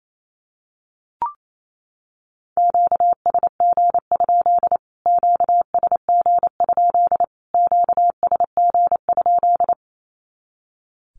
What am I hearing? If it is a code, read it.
Morse code, heard as a keyed tone, about 28 wpm, 700 Hz. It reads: QSG? QSG? QSG?
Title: Q-Codes Repeated 3x 28wpm